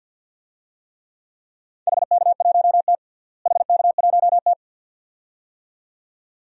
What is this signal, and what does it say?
Morse code, heard as a keyed tone, about 51 wpm, 700 Hz. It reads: HK1T HK1T
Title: DXCC-100 Call Signs - 4 Characters - Repeated 2x - 50wpm